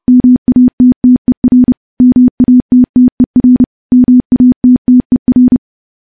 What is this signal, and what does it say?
Morse code, heard as a keyed tone, about 30 wpm, 250 Hz. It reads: MATTER MATTER MATTER
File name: morse / morse-training-30WPM